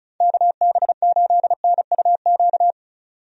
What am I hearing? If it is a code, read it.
Morse code, heard as a keyed tone, about 34 wpm, 700 Hz. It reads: KB8NUQ
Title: USA Call Signs - 6 Characters - 35wpm